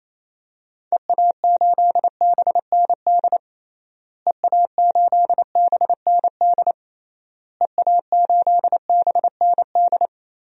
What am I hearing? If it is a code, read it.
Morse code, heard as a keyed tone, about 28 wpm, 700 Hz. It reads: EA8BND EA8BND EA8BND
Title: DXCC-100 Call Signs - 6 Characters - Repeated 3x - 28wpm